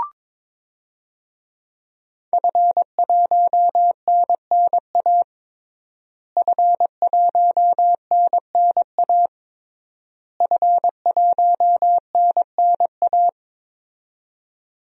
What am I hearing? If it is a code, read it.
Morse code, heard as a keyed tone, about 22 wpm, 700 Hz. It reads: F1NNA F1NNA F1NNA
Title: DXCC-100 Call Signs - 5 Characters - Repeated 3x - 22wpm